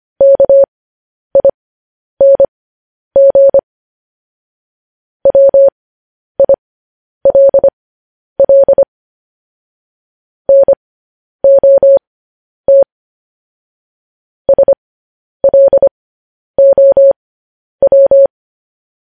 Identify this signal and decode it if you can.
Morse code, heard as a keyed tone, about 25 wpm, spaced as for 10 wpm, 550 Hz. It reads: KING WILL NOT SLOW
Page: news headlines in morse code at 10 wpm